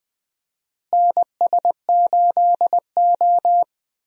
Morse code, heard as a keyed tone, about 20 wpm, 700 Hz. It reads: NS8O
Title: USA Call Signs - 4 Characters - Repeated 2x - 20wpm